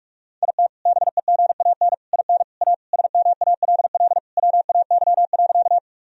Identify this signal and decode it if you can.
Morse code, heard as a keyed tone, about 41 wpm, 700 Hz. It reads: IT BEGAN IN A SMALL WAY.
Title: Sentences from Top 1000 Words - Review Part 1 - 45wpm